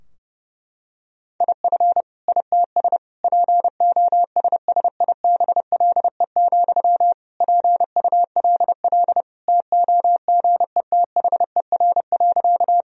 Morse code, heard as a keyed tone, about 30 wpm, 700 Hz. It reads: IF ITS POSSIBLE, PULL TOGETHER.
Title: Sentences from Top 1000 Words - Review Part 2 - 30wpm